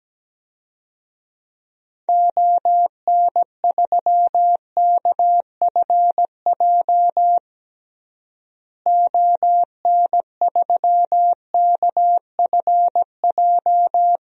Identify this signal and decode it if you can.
Morse code, heard as a keyed tone, about 17 wpm, 700 Hz. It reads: ON3KFJ ON3KFJ
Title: DXCC-100 Call Signs - 6 Characters - Repeated 3x - 17wpm